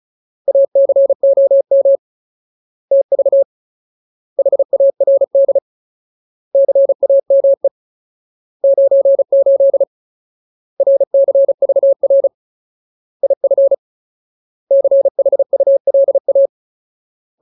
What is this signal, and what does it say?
Morse code, heard as a keyed tone, about 34 wpm, 550 Hz. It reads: ACOM TV HARD CAME 98 RCVR IF CHULA
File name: Ultimate Word List / 35WPM Ultimate